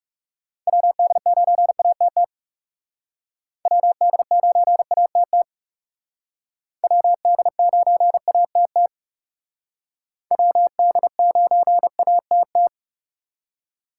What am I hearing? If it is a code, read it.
Morse code, heard as a keyed tone, about 36 wpm, 700 Hz. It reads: WD9ATT WD9ATT WD9ATT WD9ATT
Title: Speed Racing - USA 2x3 Call Signs 30wpm